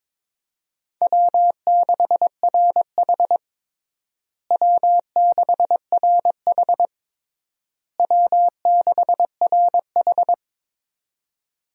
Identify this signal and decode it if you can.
Morse code, heard as a keyed tone, about 21 wpm, 700 Hz. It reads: W6RH W6RH W6RH
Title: USA Call Signs - 4 Characters - Repeated 3x - 22wpm